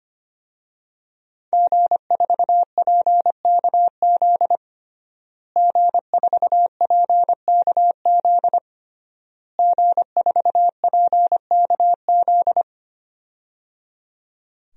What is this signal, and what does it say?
Morse code, heard as a keyed tone, about 25 wpm, 700 Hz. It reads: G4PKZ G4PKZ G4PKZ
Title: DXCC-100 Call Signs - 5 Characters - Repeated 3x - 25wpm